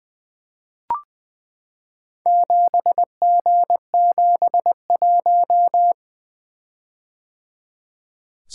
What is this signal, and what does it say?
Morse code, heard as a keyed tone, about 20 wpm, 700 Hz. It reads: 7G71
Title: Four Letter-Number Combinations - 20wpm